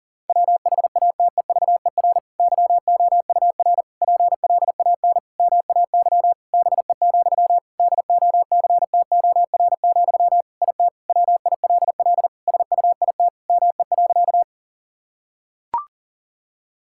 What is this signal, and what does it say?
Morse code, heard as a keyed tone, about 40 wpm, 700 Hz. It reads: WHATEVER YOUR PLAN MAY BE, DOCTOR, IT WILL SUIT ME.
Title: Sentences from Top 1600 Words 40wpm